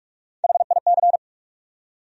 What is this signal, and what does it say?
Morse code, heard as a keyed tone, about 42 wpm, 700 Hz. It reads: HIC
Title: USA Call Sign Postfixes - 3 Letters 45wpm